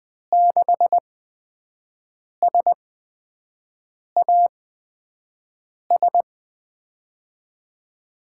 Morse code, heard as a keyed tone, about 21 wpm, spaced as for 5 wpm, 700 Hz. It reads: 6SAS
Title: USA Call Signs - 6 Characters - 8x Character Spacing - 20wpm